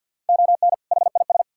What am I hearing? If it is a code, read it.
Morse code, heard as a keyed tone, about 50 wpm, 700 Hz. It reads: ON HIS
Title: Sentences from Top 1600 Words 50wpm